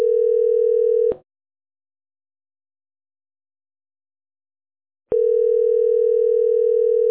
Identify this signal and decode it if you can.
Ringback tone (North America), 440 + 480 Hz.